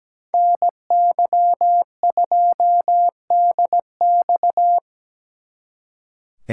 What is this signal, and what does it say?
Morse code, heard as a keyed tone, about 19 wpm, 700 Hz. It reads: NY2DX
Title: Speed Racing - USA 2x2 Call Signs 17wpm